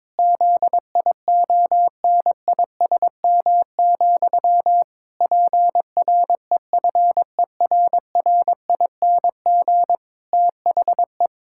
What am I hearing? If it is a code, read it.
Morse code, heard as a keyed tone, about 22 wpm, 700 Hz. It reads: ZIONISM, PREFERRING THE
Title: Pangram Challenge v2 - 22wpm